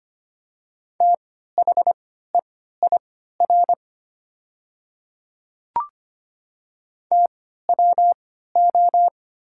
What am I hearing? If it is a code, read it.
Morse code, heard as a keyed tone, about 25 wpm, spaced as for 14 wpm, 700 Hz. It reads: THEIR TWO
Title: 3x Character Spacing - Top 100 Words - 25wpm